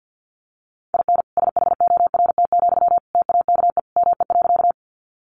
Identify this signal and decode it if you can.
Morse code, heard as a keyed tone, about 50 wpm, 700 Hz. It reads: IN SHORT, TAKE ME.
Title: Sentences from Top 400 Words 50wpm